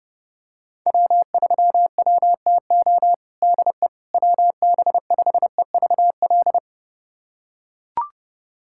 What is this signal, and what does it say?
Morse code, heard as a keyed tone, about 30 wpm, 700 Hz. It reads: W3WTO DE WB5EVL
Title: USA Callsign DE Callsign 30wpm